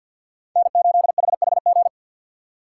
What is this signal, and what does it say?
Morse code, heard as a keyed tone, about 49 wpm, 690 Hz. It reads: N8HHG
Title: K1USN SST QSO Practice - 50wpm